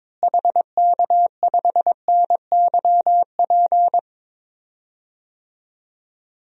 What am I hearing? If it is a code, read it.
Morse code, heard as a keyed tone, about 22 wpm, 700 Hz. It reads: HK5NYP
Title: DXCC-100 Call Signs - 6 Characters - Repeated 2x - 22wpm